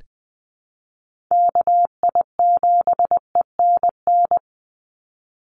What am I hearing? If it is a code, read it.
Morse code, heard as a keyed tone, about 20 wpm, 700 Hz. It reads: KI7ENN
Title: USA Call Signs 20wpm